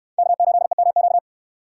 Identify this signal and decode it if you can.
Morse code, heard as a keyed tone, about 69 wpm, 700 Hz. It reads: K8EMQ